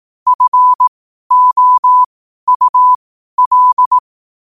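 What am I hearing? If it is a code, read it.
Morse code, heard as a keyed tone, about 18 wpm, spaced as for 12 wpm, 1000 Hz. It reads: FOUL
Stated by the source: Morse code quotes at 12 words per minute.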